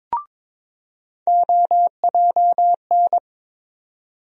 Morse code, heard as a keyed tone, about 22 wpm, 700 Hz. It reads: OJN
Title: USA Call Sign Postfixes - 3 Letters 22wpm